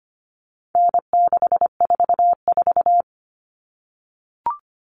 Morse code, heard as a keyed tone, about 25 wpm, 700 Hz. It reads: N644
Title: All Cut Numbers - 4-Digit 25wpm